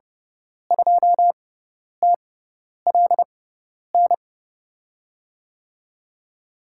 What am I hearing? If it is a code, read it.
Morse code, heard as a keyed tone, about 30 wpm, spaced as for 10 wpm, 700 Hz. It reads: 2TLN
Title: USA Call Signs - 5 Characters - 6x Character Spacing - 30wpm